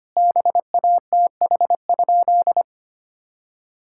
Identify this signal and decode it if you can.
Morse code, heard as a keyed tone, about 25 wpm, 700 Hz. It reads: BATH?
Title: Sentences from Top 1600 Words 25wpm